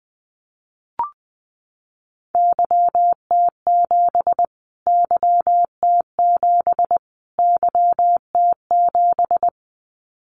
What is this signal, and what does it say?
Morse code, heard as a keyed tone, about 20 wpm, 700 Hz. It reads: YT7 YT7 YT7
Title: Callsign Prefixes for Top 25 DXCC - Repeated 3x - 20wpm